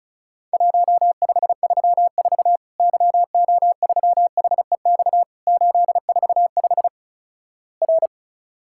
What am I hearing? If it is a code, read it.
Morse code, heard as a keyed tone, about 36 wpm, 700 Hz. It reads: 1534 YO3HEX 845 R
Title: Worked All Europe DX Contest - QTC Exchange - 35wpm